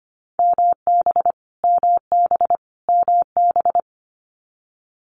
Morse code, heard as a keyed tone, about 25 wpm, 700 Hz. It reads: MB MB MB
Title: Multi-letter Phonemes Repeated 3x 25wpm